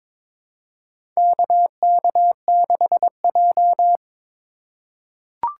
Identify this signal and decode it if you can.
Morse code, heard as a keyed tone, about 21 wpm, 700 Hz. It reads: KK6J
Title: USA Call Signs - 4 Characters - 4x Character Spacing - 22wpm